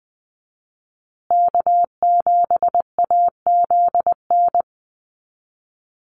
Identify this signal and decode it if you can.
Morse code, heard as a keyed tone, about 20 wpm, 700 Hz. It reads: K7AZN
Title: USA Call Signs Repeated 2x 20wpm